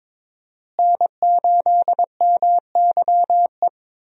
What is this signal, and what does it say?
Morse code, heard as a keyed tone, about 21 wpm, 700 Hz. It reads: N8MYE
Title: USA Call Signs - 5 Characters - Repeated 2x - 22wpm